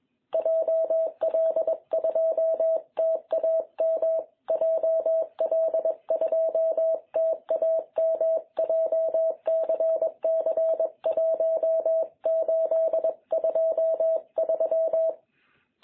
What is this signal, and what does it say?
Morse code, heard as a keyed tone, about 21 wpm, 660 Hz. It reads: JL2TAM JL2TAM JCC1823
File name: jh8cbh / 2021cw